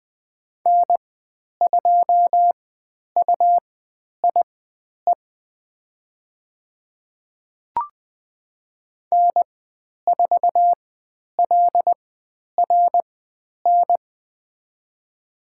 Morse code, heard as a keyed tone, about 20 wpm, spaced as for 10 wpm, 700 Hz. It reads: N2UIE N4LRN
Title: 109 Lesson 18g - Letter m - Call Signs 20wpm